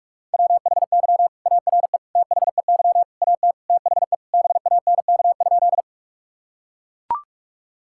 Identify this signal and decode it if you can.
Morse code, heard as a keyed tone, about 42 wpm, 700 Hz. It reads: WHY ARE THEY AT THE BANK?
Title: Sentences from Top 1000 Words - Review Part 6 - 45wpm